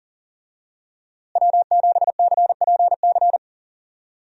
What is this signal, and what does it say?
Morse code, heard as a keyed tone, about 40 wpm, 700 Hz. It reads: W7CPC
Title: USA Call Signs - 5 Characters - Repeated 2x - 40wpm